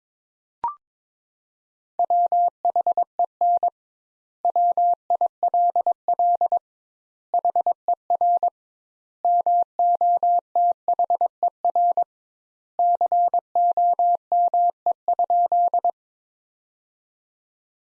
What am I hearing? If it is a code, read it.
Morse code, heard as a keyed tone, about 22 wpm, 700 Hz. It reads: WHEN WILL HER MOTHER COME?
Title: Sentences Top 200 Words - 22wpm